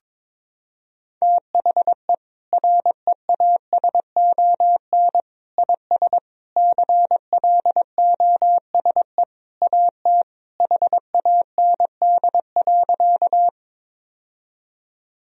Morse code, heard as a keyed tone, about 22 wpm, 700 Hz. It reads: THE REASON IS CLOSE AT HAND.
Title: Sentences from Top 1000 Words - Review Part 1 - 22wpm